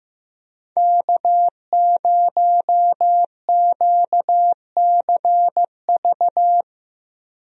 Morse code, heard as a keyed tone, about 15 wpm, 700 Hz. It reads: K0QCV
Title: USA Call Signs - 5 Characters - 15wpm